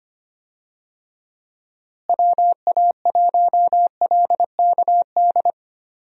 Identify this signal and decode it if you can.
Morse code, heard as a keyed tone, about 25 wpm, 700 Hz. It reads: WA1LKD
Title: USA Call Signs - 6 Characters - Repeated 3x - 25wpm